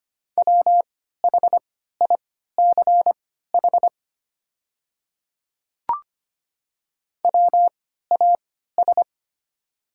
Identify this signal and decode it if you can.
Morse code, heard as a keyed tone, about 25 wpm, spaced as for 14 wpm, 700 Hz. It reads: WHICH WAS
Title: 3x Character Spacing - Top 100 Words - 25wpm